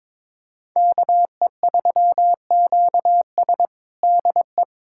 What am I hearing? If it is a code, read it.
Morse code, heard as a keyed tone, about 22 wpm, 700 Hz. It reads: KE3QS DE
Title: USA Callsign DE Callsign - 5 Characters - 22wpm